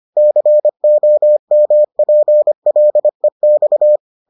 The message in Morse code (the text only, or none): COMPLEX